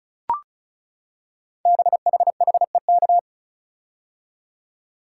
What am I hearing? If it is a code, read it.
Morse code, heard as a keyed tone, about 36 wpm, 690 Hz. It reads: BHHEK
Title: Code Groups - 5 Letters - 35wpm